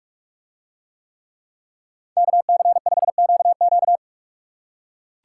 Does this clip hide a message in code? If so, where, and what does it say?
Yes: Morse code from 2.2 to 4.0 s, heard as a keyed tone, about 44 wpm, 700 Hz. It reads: KC5QQ